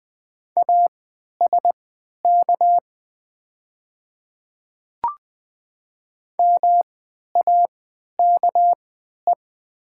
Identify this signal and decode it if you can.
Morse code, heard as a keyed tone, about 20 wpm, spaced as for 11 wpm, 700 Hz. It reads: ASK MAKE
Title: 3x Character Spacing - Top 300 Words - 20wpm